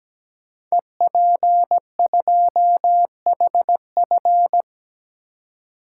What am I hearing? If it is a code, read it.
Morse code, heard as a keyed tone, about 17 wpm, 700 Hz. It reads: EP2HF
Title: Code Groups - 5 Letter:Numbers - 17wpm